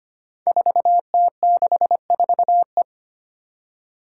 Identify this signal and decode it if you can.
Morse code, heard as a keyed tone, about 25 wpm, 700 Hz. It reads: 4T64E
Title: Five Challenging Letter-Number Combinations 25wpm